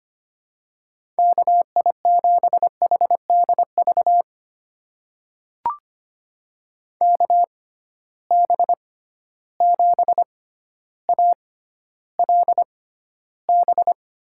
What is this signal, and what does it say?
Morse code, heard as a keyed tone, about 25 wpm, spaced as for 13 wpm, 700 Hz. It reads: KI7HDV KB7ALB
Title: USA Call Signs - 6x Character Spacing - 25wpm